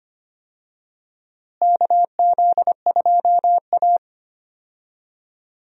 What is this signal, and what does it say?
Morse code, heard as a keyed tone, about 25 wpm, 700 Hz. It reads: KZ2A
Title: USA Call Signs - 4 Characters - Repeated 2x - 25wpm